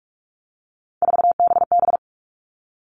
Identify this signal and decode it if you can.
Morse code, heard as a keyed tone, about 44 wpm, 700 Hz. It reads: 4BB
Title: All Cut Numbers - 3-Digit 45wpm